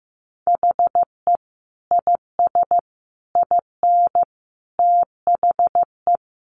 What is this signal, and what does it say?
Morse code, heard as a keyed tone, about 15 wpm, 700 Hz. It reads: HE IS IN THE
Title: Sentences from Top 500 Words 15wpm